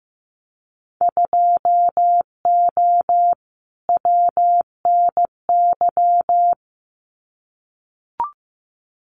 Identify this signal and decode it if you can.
Morse code, heard as a keyed tone, about 15 wpm, 700 Hz. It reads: 2O WNY
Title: Winter Field Day Exchanges Encore 15wpm